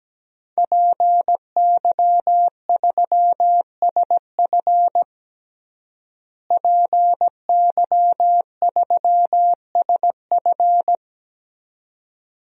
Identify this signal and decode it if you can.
Morse code, heard as a keyed tone, about 17 wpm, 700 Hz. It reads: PY3SF PY3SF
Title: DXCC-100 Call Signs - 5 Characters - Repeated 2x - 17wpm